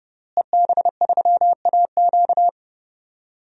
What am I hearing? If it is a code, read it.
Morse code, heard as a keyed tone, about 30 wpm, 700 Hz. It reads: EB3AQ
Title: DXCC-100 Call Signs - 5 Characters - Repeated 2x - 30wpm